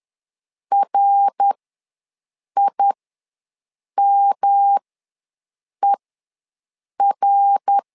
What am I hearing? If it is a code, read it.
Morse code, heard as a keyed tone, about 10 wpm, spaced as for 6 wpm, 800 Hz. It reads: RIMER
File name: LEZIONI / 3 MWRO / 3 MWR ws 10wpm 2151